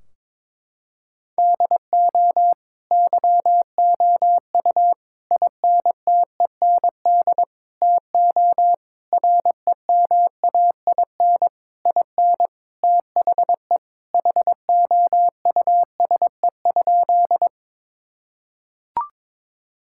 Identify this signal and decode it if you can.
Morse code, heard as a keyed tone, about 22 wpm, 700 Hz. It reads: DO YOU INTEND TO REMAIN IN THE HOUSE?
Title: Sentences from Top 1600 Words 22wpm